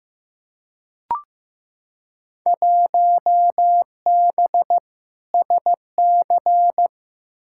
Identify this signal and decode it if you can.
Morse code, heard as a keyed tone, about 15 wpm, 700 Hz. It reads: 1B SC
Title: Field Day Exchanges Encore - 15wpm